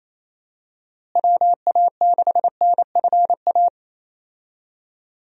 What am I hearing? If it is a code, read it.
Morse code, heard as a keyed tone, about 28 wpm, 700 Hz. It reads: WA6NFA
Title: USA Call Signs - 6 Characters - Repeated 2x - 28wpm